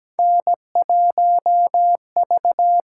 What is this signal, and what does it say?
Morse code, heard as a keyed tone, about 17 wpm, 700 Hz. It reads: N1V
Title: USA Call Signs - 4 Characters - 6x Character Spacing - 17wpm